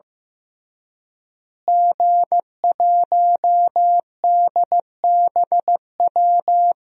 Morse code, heard as a keyed tone, about 15 wpm, 700 Hz. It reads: G1DBW
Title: DXCC-100 Call Signs - 5 Characters - Repeated 3x - 15wpm